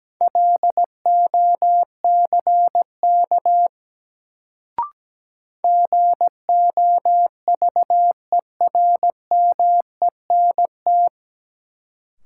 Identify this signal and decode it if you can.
Morse code, heard as a keyed tone, about 17 wpm, 700 Hz. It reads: LOCK GOVERMENT
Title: Sentences from Top 1300 Words - New Word Drill - 17wpm